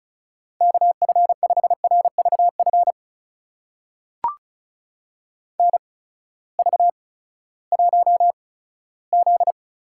Morse code, heard as a keyed tone, about 36 wpm, spaced as for 18 wpm, 700 Hz. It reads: KF5RVF NV1Z
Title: USA Call Signs - 8x Character Spacing - 35wpm